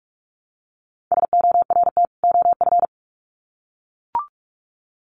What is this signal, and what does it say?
Morse code, heard as a keyed tone, about 44 wpm, 700 Hz. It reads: SORT OF
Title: Sets of 2 Words 45wpm